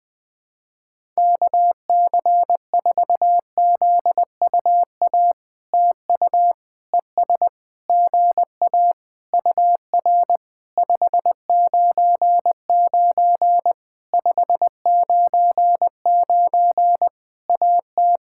Morse code, heard as a keyed tone, about 20 wpm, 700 Hz. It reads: KC4ZUA TU ES GA UR 599 599 AT
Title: POTA QSO Practice - 20wpm